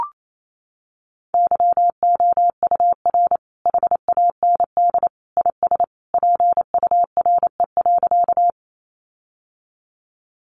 Morse code, heard as a keyed tone, about 28 wpm, 700 Hz. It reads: YOUR HAND IS PURE.
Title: Sentences from Top 1500 Words 28wpm